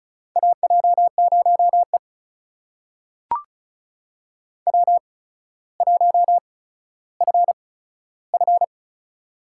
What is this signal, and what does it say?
Morse code, heard as a keyed tone, about 36 wpm, spaced as for 15 wpm, 700 Hz. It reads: AJ0E W1FF
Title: USA Call Signs - 4 Characters - 8x Character Spacing - 35wpm